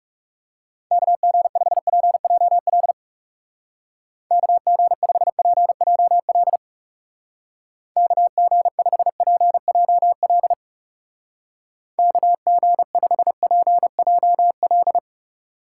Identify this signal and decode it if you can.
Morse code, heard as a keyed tone, about 37 wpm, 700 Hz. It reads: KG5PJL KG5PJL KG5PJL KG5PJL
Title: Speed Racing - USA 2x3 Call Signs 30wpm